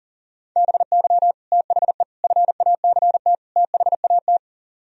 Morse code, heard as a keyed tone, about 51 wpm, 700 Hz. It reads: BY THE FACT THAT